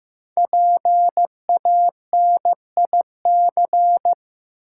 Morse code, heard as a keyed tone, about 15 wpm, 700 Hz. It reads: PANIC
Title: Five Letter Words - 15wpm